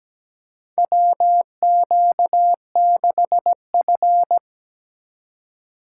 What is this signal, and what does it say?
Morse code, heard as a keyed tone, about 17 wpm, 700 Hz. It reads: WQ6F
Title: USA Call Signs - 4 Characters - 2x Character Spacing - 17wpm